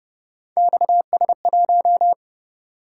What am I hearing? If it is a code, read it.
Morse code, heard as a keyed tone, about 29 wpm, 700 Hz. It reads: XS1